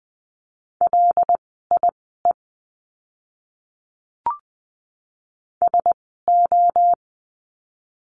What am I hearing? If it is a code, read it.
Morse code, heard as a keyed tone, about 20 wpm, spaced as for 15 wpm, 700 Hz. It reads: LIE SO
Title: 094 Lesson 16f - Letter c - Words 20wpm